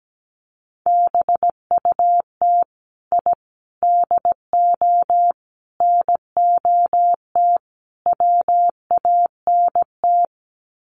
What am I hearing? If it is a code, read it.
Morse code, heard as a keyed tone, about 17 wpm, 700 Hz. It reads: BUT I DO NOT WANT
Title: Sets of 5 Words 17wpm